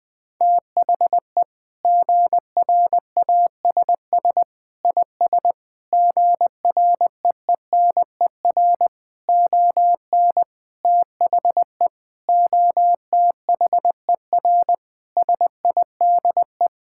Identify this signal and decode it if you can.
Morse code, heard as a keyed tone, about 20 wpm, 700 Hz. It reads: THE GRASS IS GREENER ON THE OTHER SIDE
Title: Sentences with 4-Word Phrases - 20wpm